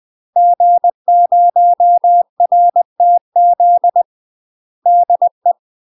Morse code, heard as a keyed tone, about 20 wpm, 700 Hz. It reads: G0RTZ DE
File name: UK MASTER QSOs TRAINING/20WPM